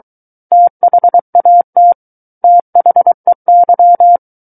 Morse code, heard as a keyed tone, about 23 wpm, 700 Hz. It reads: THAT THEY
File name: Book One / 14-Chapter 14 - 23 WPM